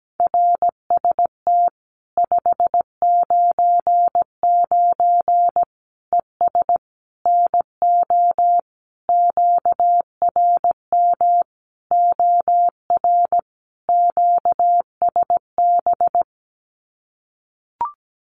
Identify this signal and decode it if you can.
Morse code, heard as a keyed tone, about 17 wpm, 700 Hz. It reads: RST 599 ES NO QRM OR QSB
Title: Common QSO Phrases 17wpm